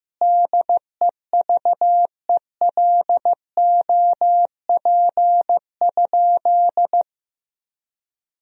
Morse code, heard as a keyed tone, about 15 wpm, 700 Hz. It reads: DEVELOP?
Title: Sentences from Top 1000 Words - Review Part 2 - 15wpm